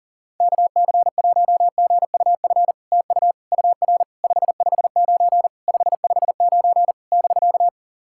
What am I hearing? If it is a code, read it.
Morse code, heard as a keyed tone, about 40 wpm, 700 Hz. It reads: KC1GUF TU UR 559 559 <BK>
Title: SOTA QSO Practice 40wpm